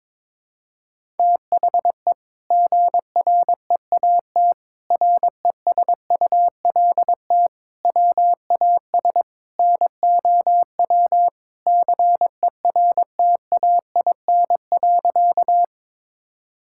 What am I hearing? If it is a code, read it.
Morse code, heard as a keyed tone, about 22 wpm, 700 Hz. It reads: THE GREAT RESULT WAS NOW CERTAIN.
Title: Sentences from Top 1000 Words - Review Part 5 - 22wpm